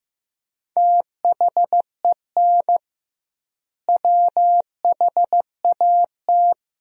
Morse code, heard as a keyed tone, about 15 wpm, 700 Hz. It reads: THEN WHAT
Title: 2x Word-Spacing - Sentences from Top 100 Words - 15wpm